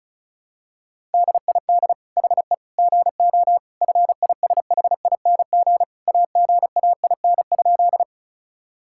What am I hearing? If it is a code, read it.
Morse code, heard as a keyed tone, about 36 wpm, 700 Hz. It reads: DID HE GO FISHING AGAIN?
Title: Sentences from Top 1000 Words - Review Part 5 - 35wpm